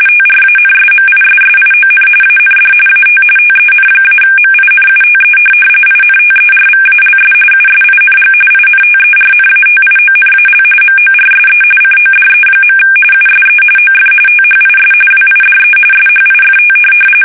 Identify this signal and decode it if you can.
RTTY, about 45.45 baud. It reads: TRANSMISSION AND AT TEN-MINUTE INTERVALS USING INTERNATIONAL MORSE CODE. USE OF THIS WIDE SHIFT
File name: fsk850 / teletype-RTTY